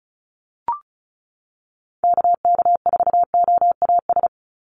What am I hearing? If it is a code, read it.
Morse code, heard as a keyed tone, about 34 wpm, 700 Hz. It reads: KK4OAS
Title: USA Call Signs 35wpm